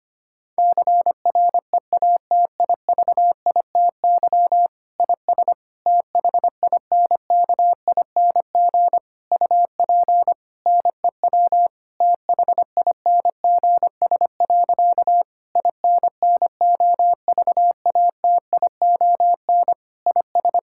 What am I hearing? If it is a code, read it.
Morse code, heard as a keyed tone, about 25 wpm, 700 Hz. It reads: CREATIVITY IS THINKING UP NEW THINGS. INNOVATION IS
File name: practice / 25wpm / Difficult Sentences with Hints - 25wpm